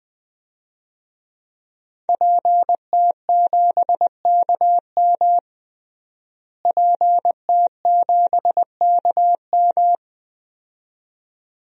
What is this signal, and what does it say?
Morse code, heard as a keyed tone, about 20 wpm, 700 Hz. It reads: PT7KM PT7KM
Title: DXCC-100 Call Signs - 5 Characters - Repeated 2x - 20wpm